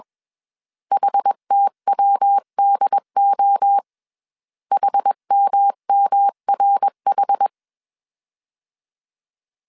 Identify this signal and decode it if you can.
Morse code, heard as a keyed tone, about 21 wpm, 800 Hz. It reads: HTWDO HMMRH
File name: CW / DET / 1 3 1500C S B 20wpm 1754